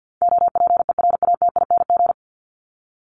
Morse code, heard as a keyed tone, about 50 wpm, 700 Hz. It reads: OPERATING
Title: ING Suffix - Top 100 Words with Base Words 50wpm